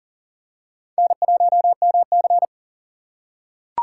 Morse code, heard as a keyed tone, about 40 wpm, 700 Hz. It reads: N1MC
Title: USA Call Signs - 4 Characters - 2x Character Spacing - 40wpm